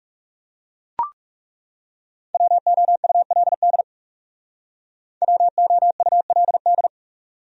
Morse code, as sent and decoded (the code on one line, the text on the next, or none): .-- --- ..- .-.. -.. / .-- --- ..- .-.. -..
WOULD WOULD